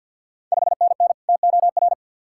Morse code, heard as a keyed tone, about 46 wpm, 690 Hz. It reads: 5NN TOR
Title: Ontario QSO Party Exchange 50wpm